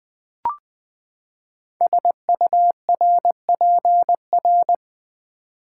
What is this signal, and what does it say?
Morse code, heard as a keyed tone, about 20 wpm, 700 Hz. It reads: SURPR
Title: Build A Word - 5 Letters At A Time - 10-Letter Words - 20wpm